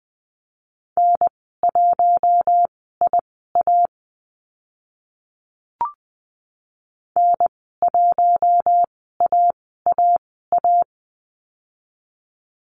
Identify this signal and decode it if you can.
Morse code, heard as a keyed tone, about 20 wpm, spaced as for 14 wpm, 700 Hz. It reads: N1IA N1AAA
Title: 062 Lesson 11g - Letter h - Call Signs 20wpm